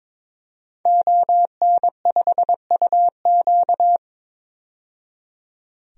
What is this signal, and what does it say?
Morse code, heard as a keyed tone, about 21 wpm, 700 Hz. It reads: ON5UQ
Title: DXCC-100 Call Signs - 5 Characters - 22wpm